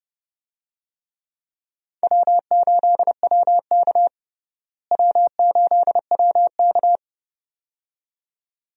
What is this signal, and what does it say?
Morse code, heard as a keyed tone, about 30 wpm, 700 Hz. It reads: W8WK W8WK
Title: USA Call Signs - 4 Characters - Repeated 2x - 30wpm